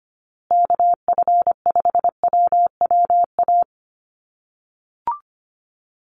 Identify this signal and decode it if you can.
Morse code, heard as a keyed tone, about 25 wpm, 700 Hz. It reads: KF5WWA
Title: USA Call Signs Repeated 3x 25wpm